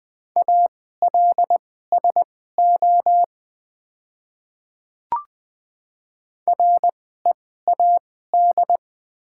Morse code, heard as a keyed tone, about 20 wpm, spaced as for 15 wpm, 700 Hz. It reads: ALSO READ
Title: 2x Character Spacing - Top 200 Words - 20wpm